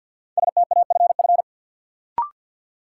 Morse code, heard as a keyed tone, about 51 wpm, 700 Hz. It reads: STAFF
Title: Sentences from Top 1200 Words - New Word Drill - 50wpm